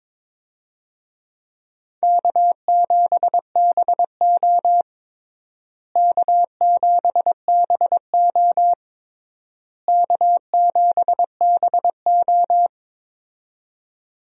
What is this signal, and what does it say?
Morse code, heard as a keyed tone, about 22 wpm, 700 Hz. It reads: K7BO K7BO K7BO
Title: USA Call Signs - 4 Characters - Repeated 3x - 22wpm